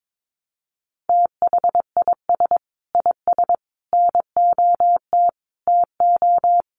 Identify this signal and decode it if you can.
Morse code, heard as a keyed tone, about 22 wpm, 700 Hz. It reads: THIS IS NOT TO
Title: Sets of 5 Words Encore 22wpm